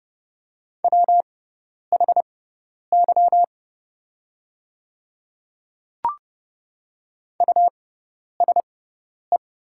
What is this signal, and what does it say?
Morse code, heard as a keyed tone, about 30 wpm, spaced as for 10 wpm, 700 Hz. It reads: WHY USE
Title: 6x Character Spacing - Top 300 Words - 30wpm